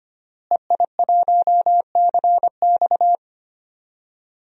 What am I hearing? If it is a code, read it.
Morse code, heard as a keyed tone, about 25 wpm, 700 Hz. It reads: EI1CX
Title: DXCC-100 Call Signs - 5 Characters - Repeated 2x - 25wpm